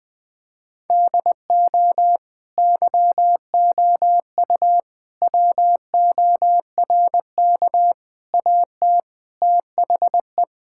Morse code, heard as a keyed tone, about 21 wpm, 700 Hz. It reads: DO YOU WORK AT THE
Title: Sentences from Top 1000 Words - Review Part 5 - 20wpm